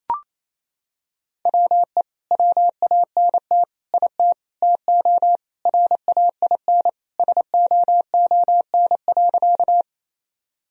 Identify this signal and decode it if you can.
Morse code, heard as a keyed tone, about 28 wpm, 700 Hz. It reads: WE WANT IT TO RAIN SOON.
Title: Sentences from Top 1000 Words - Review Part 1 - 28wpm